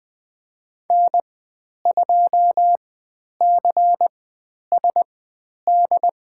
Morse code, heard as a keyed tone, about 20 wpm, spaced as for 10 wpm, 700 Hz. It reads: N2CSD
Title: 102 Lesson 17g - Letter u - Call Signs 20wpm